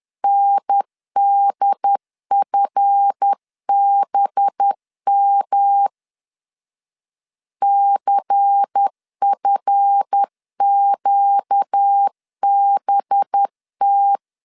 Morse code, heard as a keyed tone, about 10 wpm, 800 Hz. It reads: NDFBM CFQBT
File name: LEZIONI / 7 GF9Z / 7 GF 10wpm 2514